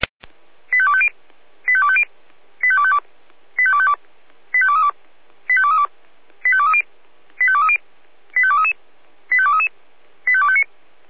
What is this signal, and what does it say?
Five-tone selective call (ZVEI), heard as quick runs of five tones, tones of about 70 ms. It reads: 85250 85250 85252 85252 85232 85232 85230 85230 85233 85233 85259